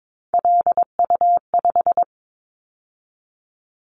Morse code, heard as a keyed tone, about 22 wpm, 700 Hz. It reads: LU5
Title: Callsign Prefixes for Top 50 DXCC - 22wpm